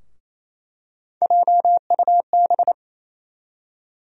Morse code, heard as a keyed tone, about 28 wpm, 700 Hz. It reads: JUB
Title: USA Call Sign Postfixes - 3 Letters 28wpm